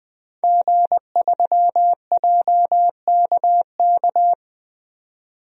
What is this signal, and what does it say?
Morse code, heard as a keyed tone, about 21 wpm, 700 Hz. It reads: G3JKK
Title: DXCC-100 Call Signs - 5 Characters - 20wpm